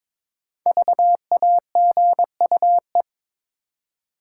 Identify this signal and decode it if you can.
Morse code, heard as a keyed tone, about 22 wpm, 700 Hz. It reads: VAGUE
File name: practice / 22wpm / 5 Letter Words - First 3 Letters Given - 22wpm